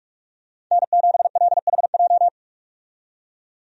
Morse code, heard as a keyed tone, about 46 wpm, 700 Hz. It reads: N7LHJ
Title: USA Call Signs - 5 Characters - 45wpm